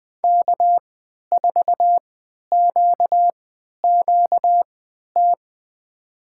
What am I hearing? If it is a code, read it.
Morse code, heard as a keyed tone, about 20 wpm, spaced as for 11 wpm, 700 Hz. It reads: K4QQT
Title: USA Call Signs - 5 Characters - 3x Character Spacing - 20wpm